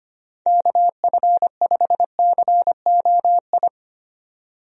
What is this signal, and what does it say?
Morse code, heard as a keyed tone, about 25 wpm, 700 Hz. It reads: KF5COI
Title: USA Call Signs - 6 Characters - 25wpm